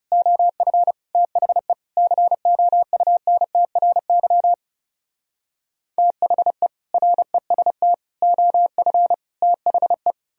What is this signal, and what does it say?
Morse code, heard as a keyed tone, about 40 wpm, 700 Hz. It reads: OF THE COUNTRY THE REST OF THE